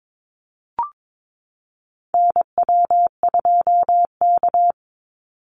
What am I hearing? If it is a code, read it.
Morse code, heard as a keyed tone, about 22 wpm, 690 Hz. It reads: NW2K
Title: Call Signs Repeated 3x - 22wpm